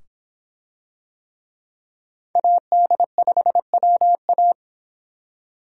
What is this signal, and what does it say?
Morse code, heard as a keyed tone, about 25 wpm, 700 Hz. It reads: AD5WA